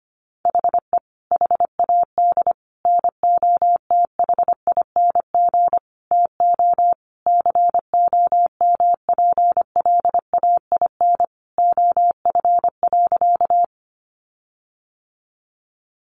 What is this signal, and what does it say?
Morse code, heard as a keyed tone, about 25 wpm, 700 Hz. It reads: HE HAD NOTHING TO COMPLAIN OF.
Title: Sentences from Top 1300 Words 25wpm